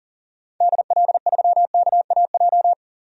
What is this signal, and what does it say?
Morse code, heard as a keyed tone, about 40 wpm, 700 Hz. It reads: DL3KAJ
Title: DXCC-100 Call Signs - 6 Characters - 40wpm